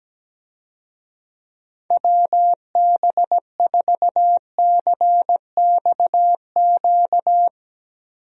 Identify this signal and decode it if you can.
Morse code, heard as a keyed tone, about 17 wpm, 700 Hz. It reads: WB4CXQ
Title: USA Call Signs - 6 Characters - Repeated 2x - 17wpm